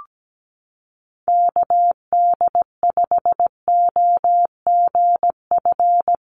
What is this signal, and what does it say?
Morse code, heard as a keyed tone, about 17 wpm, 700 Hz. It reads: KD5OGF
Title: USA Call Signs Repeated 2x 17wpm